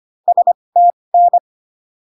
Morse code, heard as a keyed tone, about 25 wpm, spaced as for 20 wpm, 700 Hz. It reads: STN